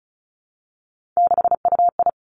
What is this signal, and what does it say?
Morse code, heard as a keyed tone, about 36 wpm, 700 Hz. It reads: 6UI
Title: Three Challenging Letter-Number Combinations 35wpm